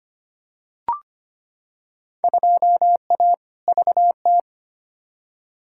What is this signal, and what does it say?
Morse code, heard as a keyed tone, about 25 wpm, 700 Hz. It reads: 2A VT
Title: Field Day Exchanges Encore - 25wpm